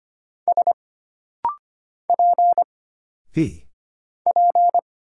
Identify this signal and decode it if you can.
Morse code, heard as a keyed tone, about 25 wpm, 700 Hz. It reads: S P P
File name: Single Character (Letter:Number:Punctuation:Prosign) - Rapid-Fire - 25wpm